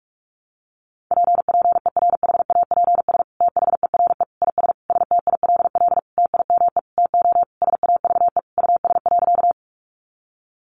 Morse code, heard as a keyed tone, about 44 wpm, 700 Hz. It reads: P PERHAPS THERE IS STILL TIME TO SAVE US.
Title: Sentences from Top 1100 Words 45wpm